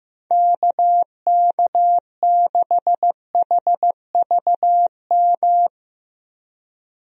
Morse code, heard as a keyed tone, about 15 wpm, 700 Hz. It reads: KK6HVM
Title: USA Call Signs - 6 Characters - 15wpm